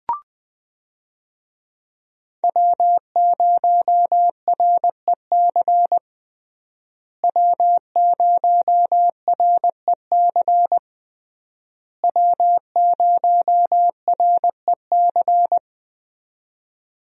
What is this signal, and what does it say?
Morse code, heard as a keyed tone, about 20 wpm, 700 Hz. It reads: W0REC W0REC W0REC
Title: USA Call Signs - 5 Characters - Repeated 3x - 20wpm